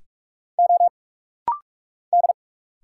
Morse code, heard as a keyed tone, about 46 wpm, 700 Hz. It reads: O D
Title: Single Letters - Rapid-Fire 45wpm